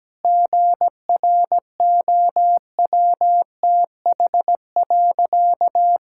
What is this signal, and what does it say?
Morse code, heard as a keyed tone, about 17 wpm, 700 Hz. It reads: GROWTH.
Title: Sentences from Top 1000 Words - Review Part 5 - 17wpm